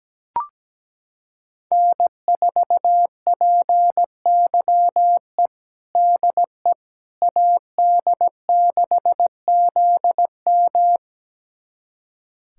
Morse code, heard as a keyed tone, about 17 wpm, 700 Hz. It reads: N4PYE DE AD6ZM
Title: USA Callsign DE Callsign - 5 Characters - 17wpm